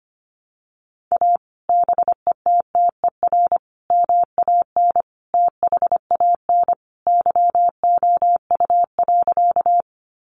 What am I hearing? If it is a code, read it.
Morse code, heard as a keyed tone, about 25 wpm, 700 Hz. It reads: A BETTER MAN THAN YOU.
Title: Sentences from Top 400 Words 25wpm